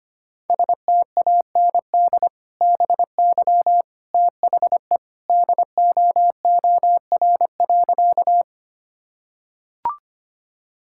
Morse code, heard as a keyed tone, about 25 wpm, 700 Hz. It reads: STAND BY THE DOOR.
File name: practice / 25wpm / Sentences with 2-Word Phrases - 25wpm